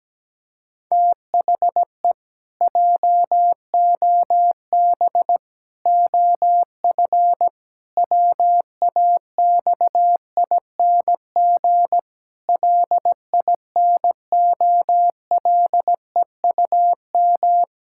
Morse code, heard as a keyed tone, about 17 wpm, 700 Hz. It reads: THE JOB OF WAXING LINOLEUM
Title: Pangram Challenge v2 - 17wpm